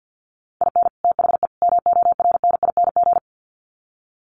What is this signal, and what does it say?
Morse code, heard as a keyed tone, about 50 wpm, 700 Hz. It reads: IN THE MORNING
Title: Sets of 3 Words 50wpm